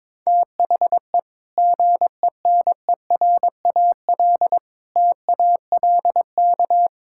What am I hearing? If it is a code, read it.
Morse code, heard as a keyed tone, about 22 wpm, 700 Hz. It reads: THE GENERAL TALK
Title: Sentences from Top 1000 Words - Review Part 5 - 22wpm